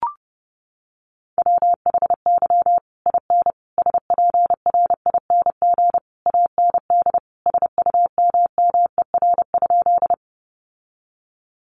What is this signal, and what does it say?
Morse code, heard as a keyed tone, about 30 wpm, 700 Hz. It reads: WHY IN SPRING AND SUMMER?
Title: Sentences from Top 1200 Words 30wpm